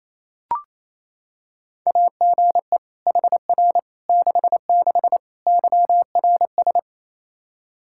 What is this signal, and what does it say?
Morse code, heard as a keyed tone, about 28 wpm, 700 Hz. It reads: AGE HR 66 YRS
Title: Practice QSO - Exchange 3 - More Info - 28wpm